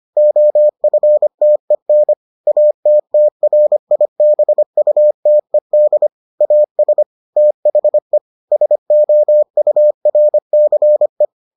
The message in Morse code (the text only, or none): OFTEN ATTRIBUTED AS THE SOURCE